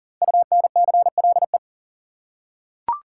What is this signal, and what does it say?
Morse code, heard as a keyed tone, about 40 wpm, 700 Hz. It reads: UNCLE
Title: Five Letter Words - 40wpm